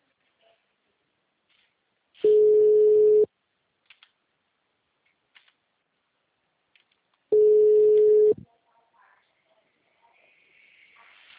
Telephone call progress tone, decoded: ringback tone

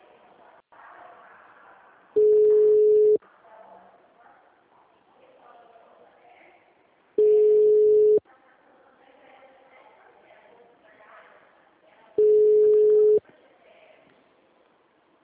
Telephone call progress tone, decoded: ringback tone

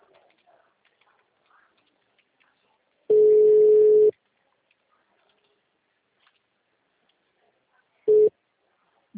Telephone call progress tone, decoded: ringback tone

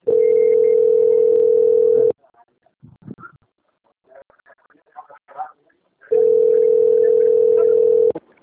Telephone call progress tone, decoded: ringback tone

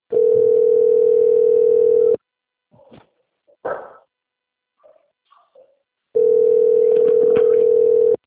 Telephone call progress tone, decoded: ringback tone

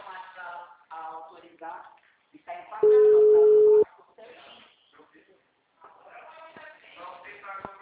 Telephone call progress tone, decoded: ringback tone

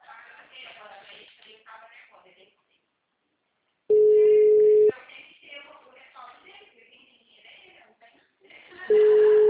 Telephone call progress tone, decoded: ringback tone

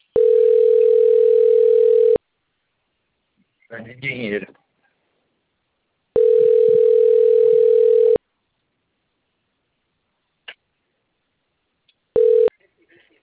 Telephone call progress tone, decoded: ringback tone

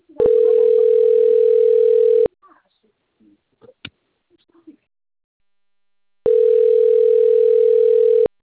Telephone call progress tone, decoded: ringback tone

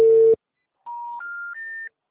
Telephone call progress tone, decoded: special information tone

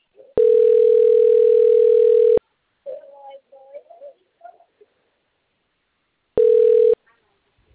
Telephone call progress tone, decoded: ringback tone

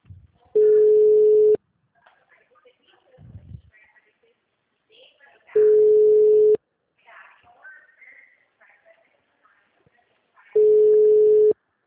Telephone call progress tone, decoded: ringback tone

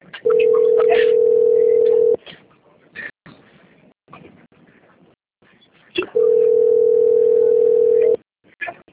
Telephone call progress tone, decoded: ringback tone

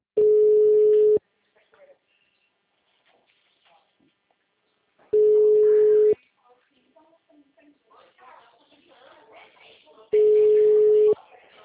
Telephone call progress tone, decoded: ringback tone

